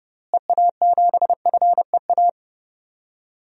Morse code, heard as a keyed tone, about 32 wpm, 700 Hz. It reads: EA7FEA
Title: DXCC-100 Call Signs - 6 Characters - Repeated 3x - 30wpm